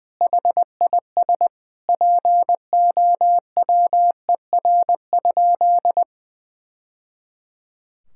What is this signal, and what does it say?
Morse code, heard as a keyed tone, about 20 wpm, 700 Hz. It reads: HIS POWER?
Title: Sentences from Top 1000 Words - Review Part 5 - 20wpm